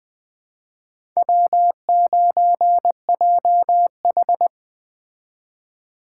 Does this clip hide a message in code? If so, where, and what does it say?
Yes: Morse code from 1.2 to 4.5 s, heard as a keyed tone, about 21 wpm, 700 Hz. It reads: W9JH